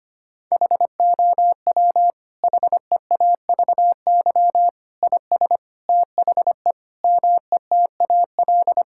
Morse code, heard as a keyed tone, about 25 wpm, 700 Hz. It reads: HOW HEAVY IS THE METAL
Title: Sentences from Top 1000 Words - Review Part 6 - 25wpm